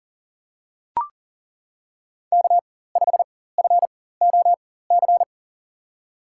Morse code, heard as a keyed tone, about 40 wpm, spaced as for 19 wpm, 700 Hz. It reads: K5FOC
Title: USA Call Signs - 5 Characters - 4x Character Spacing - 40wpm